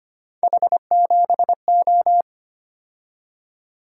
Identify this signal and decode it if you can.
Morse code, heard as a keyed tone, about 25 wpm, 700 Hz. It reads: H7O
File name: Three Letter-Number Combinations - 25wpm